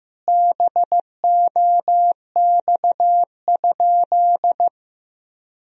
Morse code, heard as a keyed tone, about 16 wpm, 700 Hz. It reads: BOX?
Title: Sentences from Top 1000 Words - Review Part 2 - 15wpm